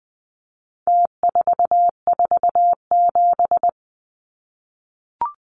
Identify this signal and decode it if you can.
Morse code, heard as a keyed tone, about 20 wpm, 700 Hz. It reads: T447
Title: Numbers - Contest Style - 4-Digit 20wpm